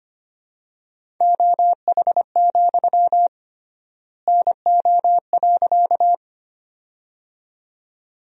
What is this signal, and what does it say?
Morse code, heard as a keyed tone, about 25 wpm, 700 Hz. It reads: OH, NO.
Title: 3x Word-Spacing - Sentences from Top 500 Words - 25wpm